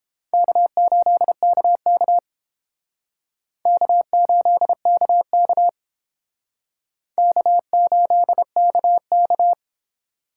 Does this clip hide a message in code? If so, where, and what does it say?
Yes: Morse code from 0.3 to 9.5 s, heard as a keyed tone, about 30 wpm, 700 Hz. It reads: K8KK K8KK K8KK